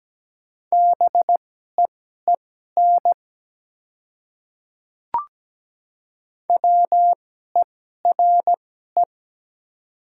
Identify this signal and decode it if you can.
Morse code, heard as a keyed tone, about 17 wpm, spaced as for 12 wpm, 700 Hz. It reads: BEEN WERE
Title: 2x Character Spacing - Top 100 Words - 17wpm